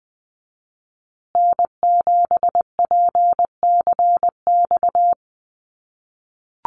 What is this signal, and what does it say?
Morse code, heard as a keyed tone, about 20 wpm, 700 Hz. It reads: N7PCX
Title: USA Call Signs Repeated 3x 20wpm